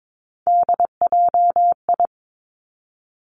Morse code, heard as a keyed tone, about 22 wpm, 700 Hz. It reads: DJI
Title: Three Challenging Letter-Number Combinations 22wpm